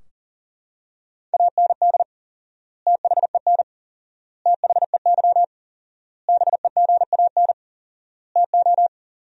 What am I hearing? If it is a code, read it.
Morse code, heard as a keyed tone, about 40 wpm, 700 Hz. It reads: AND THEN THEY BEGAN TO
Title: 4x Word-Spacing - Sentences from Top 300 Words - 40wpm